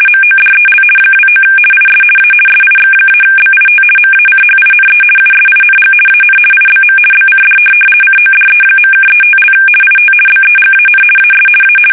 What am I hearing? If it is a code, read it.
RTTY, about 50 baud. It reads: MODEL 29 IS A SIX-BIT MACHINE USING AN IBM BCD CODE.(35) IT BEGAN AS A
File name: fsk850-50B / teletype-corp